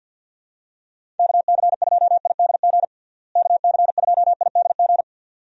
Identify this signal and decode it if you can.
Morse code, heard as a keyed tone, about 47 wpm, 700 Hz. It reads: KC2IDG KC2IDG
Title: USA Call Signs - 6 Characters - Repeated 2x - 50wpm